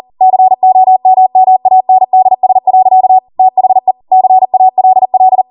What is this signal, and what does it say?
Morse code, heard as a keyed tone, about 40 wpm, 750 Hz. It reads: COMMANDS. THE CALL